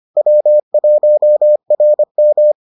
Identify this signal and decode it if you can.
Morse code, heard as a keyed tone, about 25 wpm, 600 Hz. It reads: W1RM